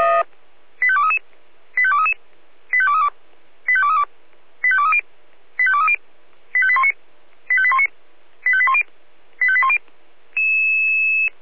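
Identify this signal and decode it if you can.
Five-tone selective call (ZVEI), heard as quick runs of five tones, tones of about 70 ms. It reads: 85233 85233 85232 85232 85230 85230 85819 85819 85810 85810
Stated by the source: ZVEI five-tone selective call 85885 received on SDR.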